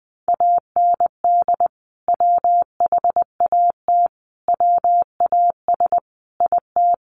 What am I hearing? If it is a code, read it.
Morse code, heard as a keyed tone, about 20 wpm, 700 Hz. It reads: AND WHAT WAS IT
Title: Sentences from Top 100 Words 20wpm